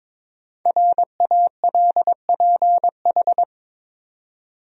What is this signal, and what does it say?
Morse code, heard as a keyed tone, about 22 wpm, 700 Hz. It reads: RALPH
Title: USA Common Names - Max 5 Letters - 22wpm